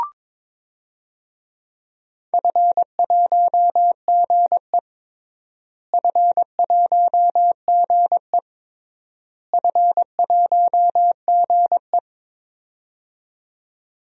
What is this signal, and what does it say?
Morse code, heard as a keyed tone, about 22 wpm, 700 Hz. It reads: F1GE F1GE F1GE
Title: DXCC-100 Call Signs - 4 Characters - Repeated 3x - 22wpm